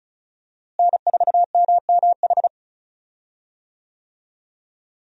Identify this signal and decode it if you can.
Morse code, heard as a keyed tone, about 36 wpm, 700 Hz. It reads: N4MMH
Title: USA Call Signs - 5 Characters - Repeated 3x - 35wpm